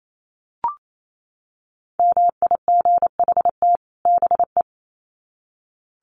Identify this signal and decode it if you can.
Morse code, heard as a keyed tone, about 28 wpm, 700 Hz. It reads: MIGHT BE
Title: Sets of 2 Words 28wpm